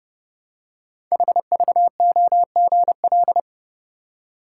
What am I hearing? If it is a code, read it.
Morse code, heard as a keyed tone, about 30 wpm, 700 Hz. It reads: HVOGL
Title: Code Groups - 5 Letter:Numbers - 30wpm